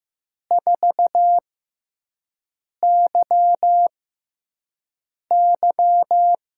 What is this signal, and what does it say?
Morse code, heard as a keyed tone, about 15 wpm, spaced as for 5 wpm, 700 Hz. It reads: 4YY
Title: USA Call Signs - 5 Characters - 6x Character Spacing - 15wpm